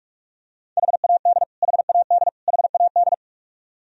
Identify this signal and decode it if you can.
Morse code, heard as a keyed tone, about 46 wpm, 700 Hz. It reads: HAD HAD HAD
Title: Top 100 Words Repeated 45wpm